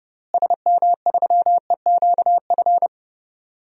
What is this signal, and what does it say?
Morse code, heard as a keyed tone, about 32 wpm, 700 Hz. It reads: SM3EQF
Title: DXCC-100 Call Signs - 6 Characters - Repeated 3x - 30wpm